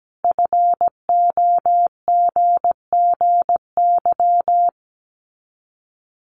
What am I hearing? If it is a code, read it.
Morse code, heard as a keyed tone, about 17 wpm, 700 Hz. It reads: FOGGY
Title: Common QSO Phrases 17wpm